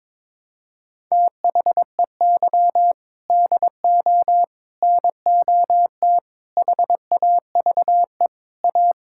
Morse code, heard as a keyed tone, about 22 wpm, 700 Hz. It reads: THEY DO NOT HAVE A
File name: practice / 22wpm / Sentences with 4-Word Phrases - 22wpm